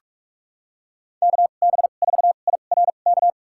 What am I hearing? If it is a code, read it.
Morse code, heard as a keyed tone, about 47 wpm, spaced as for 33 wpm, 700 Hz. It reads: KB4IRK
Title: USA Call Signs - 6 Characters - 2x Character Spacing - 45wpm